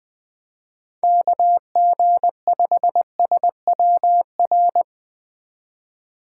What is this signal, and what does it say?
Morse code, heard as a keyed tone, about 20 wpm, 700 Hz. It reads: KG5SWR
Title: Speed Racing - USA 2x3 Call Signs 17wpm